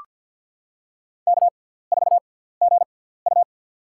Morse code, heard as a keyed tone, about 46 wpm, spaced as for 17 wpm, 700 Hz. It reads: K4GU
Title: USA Call Signs - 4 Characters - 6x Character Spacing - 50wpm